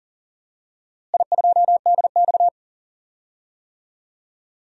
Morse code, heard as a keyed tone, about 40 wpm, 700 Hz. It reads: I2DX
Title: DXCC-100 Call Signs - 4 Characters - Repeated 2x - 40wpm